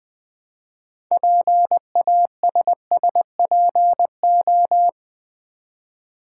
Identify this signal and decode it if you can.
Morse code, heard as a keyed tone, about 20 wpm, 700 Hz. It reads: PASSPO
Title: Build A Word - 3 Letters At A Time - 8-Letter Words - 20wpm